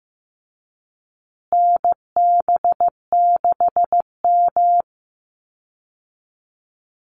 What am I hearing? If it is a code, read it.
Morse code, heard as a keyed tone, about 15 wpm, 700 Hz. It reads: NB6M
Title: Call Signs Repeated 3x - 15wpm